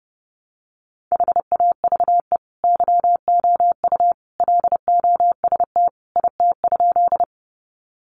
Morse code, heard as a keyed tone, about 30 wpm, 700 Hz. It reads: HAVE YOU LOST IT?
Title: Sentences from Top 1400 Words 30wpm